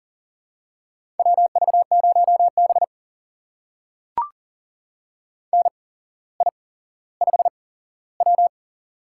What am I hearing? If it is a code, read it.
Morse code, heard as a keyed tone, about 40 wpm, spaced as for 17 wpm, 700 Hz. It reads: WV0B NI5W
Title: USA Call Signs - 8x Character Spacing - 40wpm